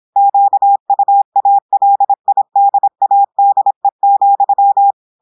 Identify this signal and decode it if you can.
Morse code, heard as a keyed tone, about 26 wpm, 800 Hz. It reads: QUALIDADE,
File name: portugais texte en clair 2 vitesse 26 mots minutes